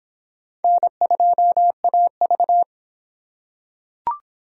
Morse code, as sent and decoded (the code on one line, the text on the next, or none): -. ..--- .- ...-
N2AV